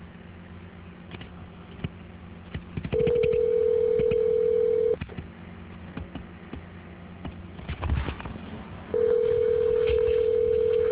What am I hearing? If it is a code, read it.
Ringback tone (North America), 440 + 480 Hz.